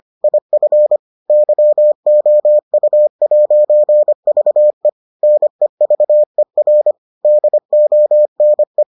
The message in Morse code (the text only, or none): IF YOU'VE NEVER DONE